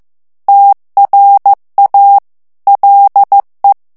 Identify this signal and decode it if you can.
Morse code, heard as a keyed tone, about 15 wpm, 800 Hz. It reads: TRA LE
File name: morse mp3 / ITALIEN texte clair 2 vitesse 14 mots minute